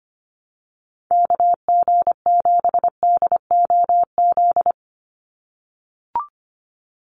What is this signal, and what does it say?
Morse code, heard as a keyed tone, about 25 wpm, 700 Hz. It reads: KG7DOZ
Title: USA Call Signs 25wpm